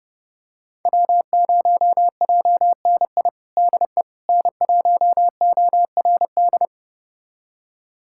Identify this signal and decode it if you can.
Morse code, heard as a keyed tone, about 32 wpm, 700 Hz. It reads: W0JNI DE N1ORD
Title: USA Callsign DE Callsign - 5 Characters - 30wpm